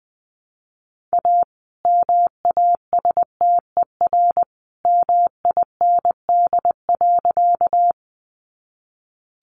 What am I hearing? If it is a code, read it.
Morse code, heard as a keyed tone, about 20 wpm, 700 Hz. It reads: A MASTER MIND.
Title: Sentences from Top 1300 Words 20wpm